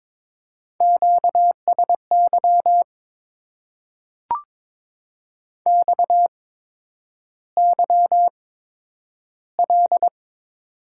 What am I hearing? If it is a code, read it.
Morse code, heard as a keyed tone, about 22 wpm, spaced as for 9 wpm, 700 Hz. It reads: QSY XYL
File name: Top 90 QSO Elements - 8x Character Spacing - 22wpm